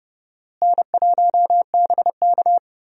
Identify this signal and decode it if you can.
Morse code, heard as a keyed tone, about 30 wpm, 700 Hz. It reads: N1BK
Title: USA Call Signs - 4 Characters - Repeated 2x - 30wpm